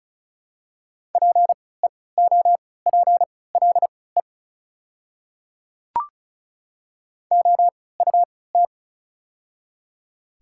Morse code, heard as a keyed tone, about 36 wpm, spaced as for 20 wpm, 700 Hz. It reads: PEOPLE OUT
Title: 3x Character Spacing - Top 200 Words - 35wpm